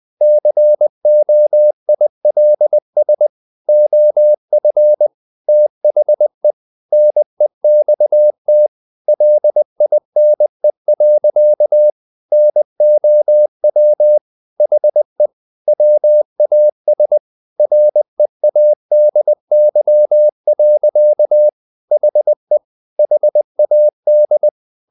Morse code, heard as a keyed tone, about 20 wpm, 600 Hz. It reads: COILS OF THE NEXT LINE. NOW HE WAS READY. HE HAD
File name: the old man and the sea / the old man and the sea part 1 20wpm